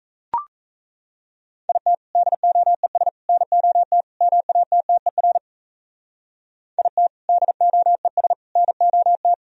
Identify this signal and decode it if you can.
Morse code, heard as a keyed tone, about 35 wpm, 700 Hz. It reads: IT DOES NOT MATTER IT DOES NOT
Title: Speed-Racing - Sets of 4 Words 28wpm